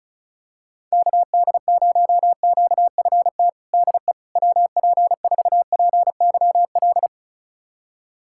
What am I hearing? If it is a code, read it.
Morse code, heard as a keyed tone, about 34 wpm, 700 Hz. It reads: KD0QFT DE WP4PYL
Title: USA Callsign DE Callsign - 6 Characters - 35wpm